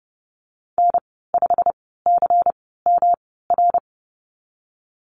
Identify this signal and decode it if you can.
Morse code, heard as a keyed tone, about 31 wpm, spaced as for 17 wpm, 700 Hz. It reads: N5CMR
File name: Lesson 19g - Letter w - Call Signs 30wpm